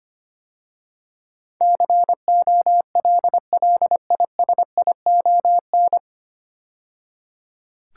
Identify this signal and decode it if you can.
Morse code, heard as a keyed tone, about 25 wpm, 700 Hz. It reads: COLLISION
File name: practice / 25wpm / 9 Letter Words - First 3 Letters Given - 25wpm